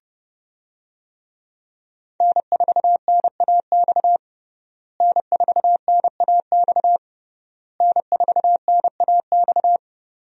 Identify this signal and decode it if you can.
Morse code, heard as a keyed tone, about 30 wpm, 700 Hz. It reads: N4NAX N4NAX N4NAX
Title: USA Call Signs - 5 Characters - Repeated 3x - 30wpm